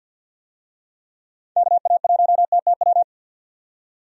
Morse code, heard as a keyed tone, about 52 wpm, 700 Hz. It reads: KA1TTW
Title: USA Call Signs - 6 Characters - 50wpm